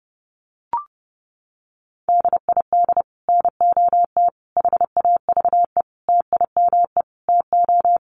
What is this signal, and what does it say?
Morse code, heard as a keyed tone, about 30 wpm, 700 Hz. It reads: DID NOT HAVE TIME TO
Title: Sets of 5 Words Encore 30wpm